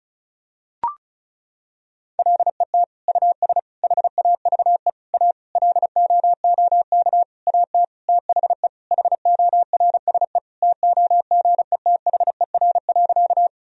Morse code, heard as a keyed tone, about 36 wpm, 700 Hz. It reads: LET US HAVE A LOOK AT THE HORSE TOGETHER.
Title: Sentences from Top 1000 Words - Review Part 1 - 35wpm